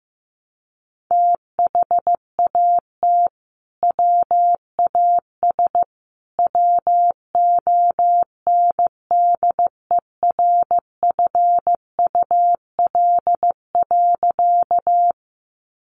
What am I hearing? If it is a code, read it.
Morse code, heard as a keyed tone, about 15 wpm, 700 Hz. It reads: THAT WAS WONDERFUL.
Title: Sentences from Top 1100 Words 15wpm